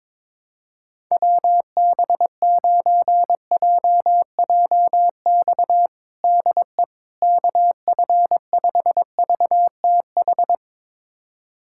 Morse code, heard as a keyed tone, about 22 wpm, 700 Hz. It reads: WB9JJX DE KF5VTH
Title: USA Callsign DE Callsign - 6 Characters - 22wpm